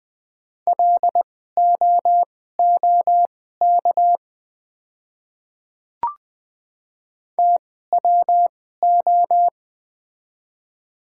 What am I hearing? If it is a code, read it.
Morse code, heard as a keyed tone, about 20 wpm, spaced as for 15 wpm, 690 Hz. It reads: LOOK TWO
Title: 2x Character Spacing - Top 100 Words - 20wpm